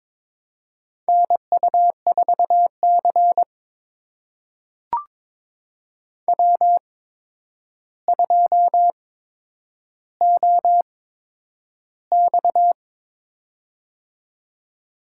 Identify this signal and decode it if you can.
Morse code, heard as a keyed tone, about 23 wpm, spaced as for 10 wpm, 700 Hz. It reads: NU4C W2OX
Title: USA Call Signs - 4 Characters - 8x Character Spacing - 22wpm